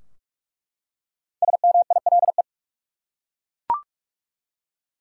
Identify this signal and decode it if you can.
Morse code, heard as a keyed tone, about 42 wpm, 700 Hz. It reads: SMILE
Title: Five Letter Words - 45wpm